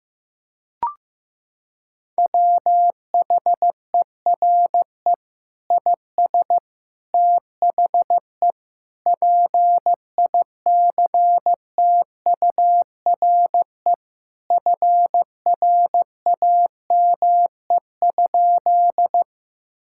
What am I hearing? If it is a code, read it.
Morse code, heard as a keyed tone, about 15 wpm, 700 Hz. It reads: WHERE IS THE PICTURE FRAME?
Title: Sentences from Top 1000 Words - Review Part 6 - 15wpm